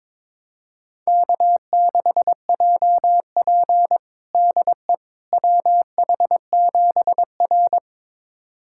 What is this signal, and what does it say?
Morse code, heard as a keyed tone, about 22 wpm, 700 Hz. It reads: K6JP DE WH7R
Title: USA Callsign DE Callsign - 4 Characters - 22wpm